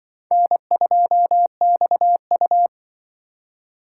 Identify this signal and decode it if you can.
Morse code, heard as a keyed tone, about 25 wpm, 700 Hz. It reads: N2XU